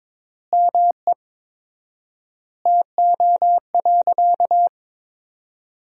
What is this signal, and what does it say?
Morse code, heard as a keyed tone, about 22 wpm, 700 Hz. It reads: ME TO.
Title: 4x Word-Spacing - Sentences from Top 400 Words - 22wpm